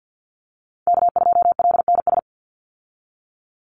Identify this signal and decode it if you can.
Morse code, heard as a keyed tone, about 49 wpm, 700 Hz. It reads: K2LNS
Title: Call Signs Repeated 3x - 50wpm